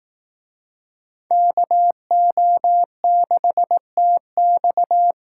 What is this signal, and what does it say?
Morse code, heard as a keyed tone, about 19 wpm, 700 Hz. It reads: KO6TX